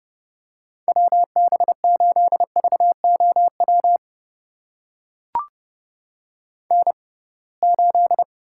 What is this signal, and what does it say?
Morse code, heard as a keyed tone, about 32 wpm, spaced as for 23 wpm, 700 Hz. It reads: WB8VOW N8
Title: USA Call Signs - 6x Character Spacing - 30wpm